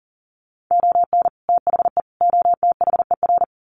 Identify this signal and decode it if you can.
Morse code, heard as a keyed tone, about 40 wpm, 700 Hz. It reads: ON THE OTHER
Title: Sets of 3 Words 40wpm